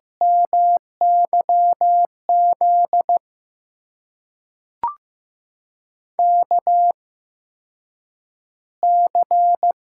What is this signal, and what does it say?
Morse code, heard as a keyed tone, about 15 wpm, spaced as for 8 wpm, 700 Hz. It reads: MYZ KC
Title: USA Call Signs - 6 Characters - 8x Character Spacing - 15wpm